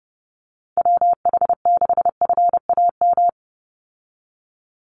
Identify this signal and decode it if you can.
Morse code, heard as a keyed tone, about 30 wpm, 700 Hz. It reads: WH6FAM
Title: USA Call Signs 30wpm